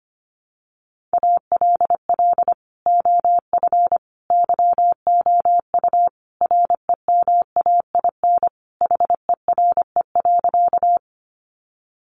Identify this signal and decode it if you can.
Morse code, heard as a keyed tone, about 25 wpm, 700 Hz. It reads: ALL OF YOU REMAIN HERE.
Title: Sentences from Top 1200 Words 25wpm